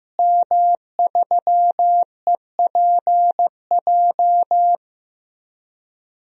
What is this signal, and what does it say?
Morse code, heard as a keyed tone, about 15 wpm, 700 Hz. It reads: M3EPJ
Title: DXCC-100 Call Signs - 5 Characters - Repeated 2x - 15wpm